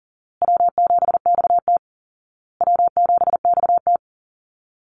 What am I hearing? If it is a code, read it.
Morse code, heard as a keyed tone, about 40 wpm, 700 Hz. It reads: W7XT W7XT
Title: Call Signs Repeated 2x - 40wpm